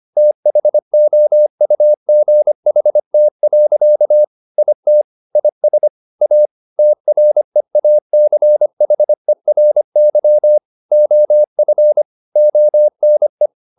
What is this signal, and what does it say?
Morse code, heard as a keyed tone, about 25 wpm, 600 Hz. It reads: THOUGHT. IT IS A TREACHERY OF ONE
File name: the old man and the sea part 1 25wpm